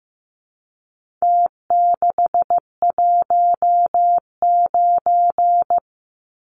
Morse code, heard as a keyed tone, about 15 wpm, 700 Hz. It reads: T619
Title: Numbers - Contest Style - 4-Digit 15wpm